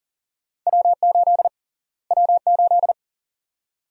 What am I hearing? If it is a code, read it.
Morse code, heard as a keyed tone, about 40 wpm, 700 Hz. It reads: W8 W8
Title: USA Call Sign Prefixes - Repeated 2x - 40wpm